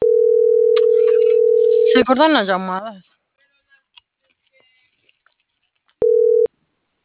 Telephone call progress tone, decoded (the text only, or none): ringback tone